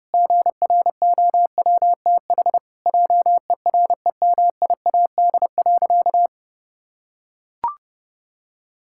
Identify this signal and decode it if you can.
Morse code, heard as a keyed tone, about 32 wpm, 700 Hz. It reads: GROWTH JEREMIAD.
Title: Pangram Challenge v2 - 30wpm